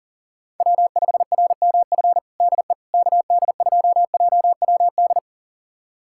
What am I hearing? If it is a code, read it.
Morse code, heard as a keyed tone, about 40 wpm, 700 Hz. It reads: W5RMF DE KD2JWD
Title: USA Callsign DE Callsign 40wpm